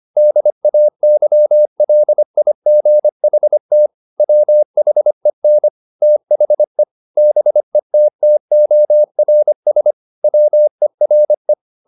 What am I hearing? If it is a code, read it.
Morse code, heard as a keyed tone, about 25 wpm, 600 Hz. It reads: DAYLIGHT WHEN THE BETTORS WERE